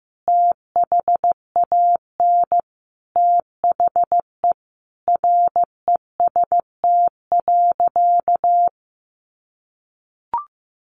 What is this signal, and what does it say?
Morse code, heard as a keyed tone, about 15 wpm, 700 Hz. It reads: THAN THE REST.
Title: Sentences from Top 1400 Words 15wpm